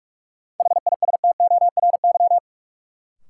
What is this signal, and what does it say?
Morse code, heard as a keyed tone, about 41 wpm, 700 Hz. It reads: HISTORY
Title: Sentences from Top 600 Words - New Word Drill - 45wpm